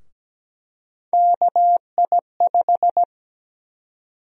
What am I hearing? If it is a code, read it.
Morse code, heard as a keyed tone, about 17 wpm, 700 Hz. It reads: KI5
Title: USA Call Sign Prefixes - 17wpm